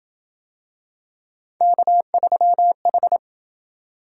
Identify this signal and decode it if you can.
Morse code, heard as a keyed tone, about 27 wpm, 700 Hz. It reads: K3H